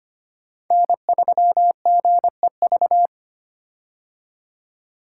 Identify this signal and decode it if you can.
Morse code, heard as a keyed tone, about 25 wpm, 700 Hz. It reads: N3GEV
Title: Code Groups - 5 Letter:Numbers - 25wpm